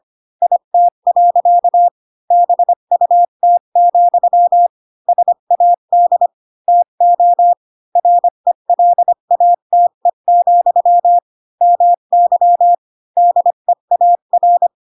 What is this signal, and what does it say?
Morse code, heard as a keyed tone, about 25 wpm, 700 Hz. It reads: IT. BUT, SAD TO RELATE, MY DEAR
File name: Pinocchio / Pinocchio ch4 20 25 ik1hge